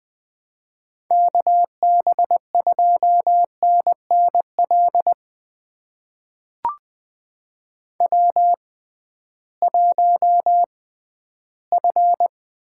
Morse code, heard as a keyed tone, about 20 wpm, spaced as for 13 wpm, 700 Hz. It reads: KB2NNL W1F
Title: USA Call Signs - 6x Character Spacing - 20wpm